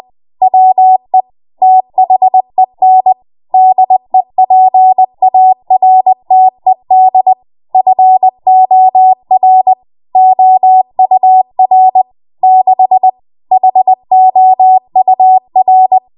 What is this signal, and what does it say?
Morse code, heard as a keyed tone, about 20 wpm, 750 Hz. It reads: WE THEN DEPARTED FOR OUR 6 HOUR
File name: Morse / 20WPM